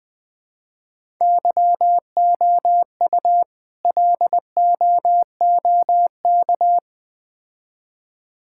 Morse code, heard as a keyed tone, about 20 wpm, 700 Hz. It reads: YOU LOOK
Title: Build A Phrase - 3-Word Phrases - 20wpm